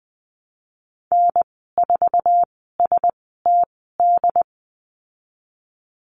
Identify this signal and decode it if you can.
Morse code, heard as a keyed tone, about 20 wpm, spaced as for 15 wpm, 700 Hz. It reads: N4STD
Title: 082 Lesson 14g - Number 2 - Call Signs 20wpm